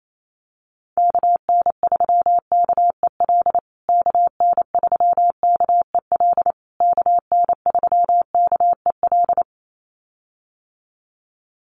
Morse code, heard as a keyed tone, about 28 wpm, 700 Hz. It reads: KN3KEL KN3KEL KN3KEL
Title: USA Call Signs Repeated 3x 28wpm